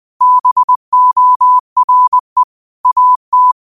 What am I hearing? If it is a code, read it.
Morse code, heard as a keyed tone, about 20 wpm, 1000 Hz. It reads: BORE AT
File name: qotdfiles / 20wpm